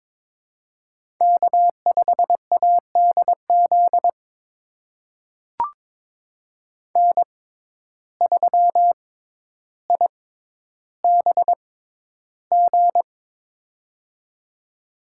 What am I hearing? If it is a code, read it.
Morse code, heard as a keyed tone, about 22 wpm, spaced as for 11 wpm, 700 Hz. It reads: K5ADZ N3IBG
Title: USA Call Signs - 5 Characters - 6x Character Spacing - 22wpm